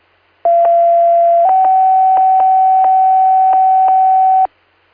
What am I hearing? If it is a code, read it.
Two-tone page, heard as a one-second tone, then a three-second tone, about 680 then 760 Hz. It is heard from 0.4 to 4.5 s.
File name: Inlet EMS